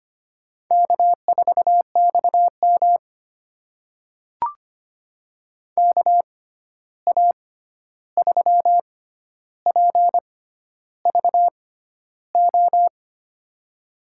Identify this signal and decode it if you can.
Morse code, heard as a keyed tone, about 25 wpm, spaced as for 11 wpm, 700 Hz. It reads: K4XM KA3PVO
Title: USA Call Signs - 6x Character Spacing - 25wpm